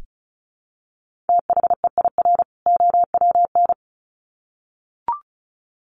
Morse code, heard as a keyed tone, about 34 wpm, 700 Hz. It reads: THEIR OWN
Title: Sets of 2 Words 35wpm